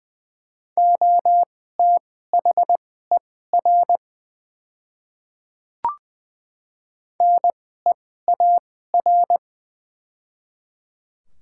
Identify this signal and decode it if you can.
Morse code, heard as a keyed tone, about 21 wpm, spaced as for 15 wpm, 700 Hz. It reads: OTHER NEAR
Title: 2x Character Spacing - Top 300 Words - 20wpm